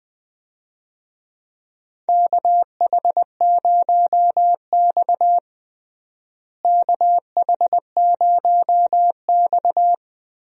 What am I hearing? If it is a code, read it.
Morse code, heard as a keyed tone, about 20 wpm, 700 Hz. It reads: KH0X KH0X
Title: USA Call Signs - 4 Characters - Repeated 2x - 20wpm